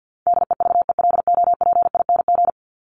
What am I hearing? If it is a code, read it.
Morse code, heard as a keyed tone, about 48 wpm, 700 Hz. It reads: DEVELOPING
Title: ING Suffix - Top 100 Words with Base Words 50wpm